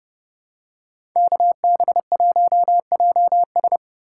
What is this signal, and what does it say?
Morse code, heard as a keyed tone, about 30 wpm, 700 Hz. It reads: KB1JS
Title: USA Call Signs - 5 Characters - 2x Character Spacing - 30wpm